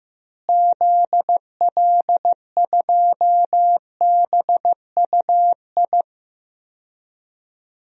Morse code, heard as a keyed tone, about 15 wpm, 700 Hz. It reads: ZL2BUI
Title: DXCC-100 Call Signs - 6 Characters - Repeated 2x - 15wpm